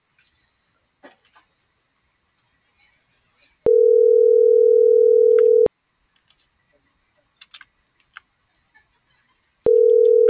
Ringback tone (North America), 440 + 480 Hz.